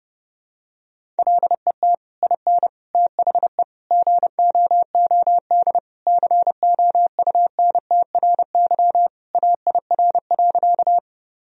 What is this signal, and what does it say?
Morse code, heard as a keyed tone, about 32 wpm, 700 Hz. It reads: LET IN THE GOOD COUNTRY AIR.
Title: Sentences from Top 1000 Words - Review Part 1 - 30wpm